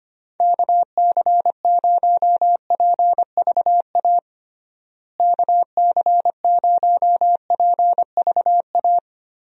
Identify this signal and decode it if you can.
Morse code, heard as a keyed tone, about 25 wpm, 700 Hz. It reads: KC0PVA KC0PVA
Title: USA Call Signs - 6 Characters - Repeated 2x - 25wpm